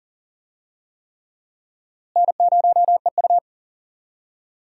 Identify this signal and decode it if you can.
Morse code, heard as a keyed tone, about 40 wpm, 700 Hz. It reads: N0EU
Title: USA Call Signs - 4 Characters - Repeated 3x - 40wpm